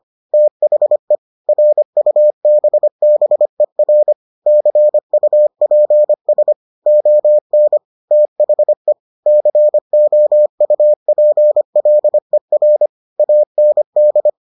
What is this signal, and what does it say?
Morse code, heard as a keyed tone, about 25 wpm, 600 Hz. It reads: THE RUBBER CUPS ON THE COUPLER AND